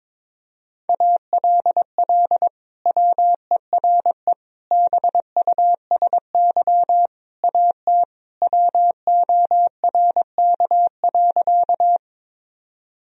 Morse code, heard as a keyed tone, about 22 wpm, 700 Hz. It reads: ALL WERE BUSY AT WORK.
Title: Sentences from Top 1000 Words - Review Part 3 - 22wpm